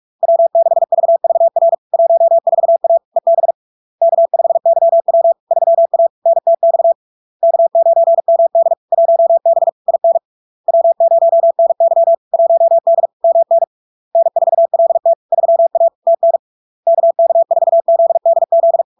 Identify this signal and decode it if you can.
Morse code, heard as a keyed tone, about 46 wpm, 670 Hz. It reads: W6VVR 14A EB K5YW 3A NTX K9MD 1B IN W0NY 1D MN N4LT 3A TN KK4ZDZ
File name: LARK-FD-2022-Log-in-CW-45wpm-670-Hz